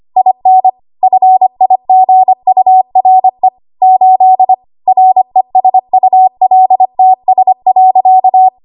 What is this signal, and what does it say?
Morse code, heard as a keyed tone, about 25 wpm, 750 Hz. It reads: IN FIGURE 8 RESULTS.
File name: Archive / 25 WPM / 25WPM